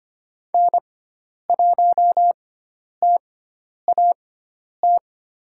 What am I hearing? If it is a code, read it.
Morse code, heard as a keyed tone, about 25 wpm, spaced as for 10 wpm, 700 Hz. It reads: N1TAT
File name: Lesson 16g - Letter c - Call Signs 25wpm